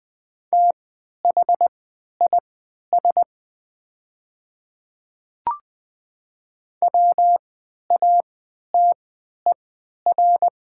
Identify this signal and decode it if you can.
Morse code, heard as a keyed tone, about 20 wpm, spaced as for 11 wpm, 700 Hz. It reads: THIS WATER
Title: 3x Character Spacing - Top 100 Words - 20wpm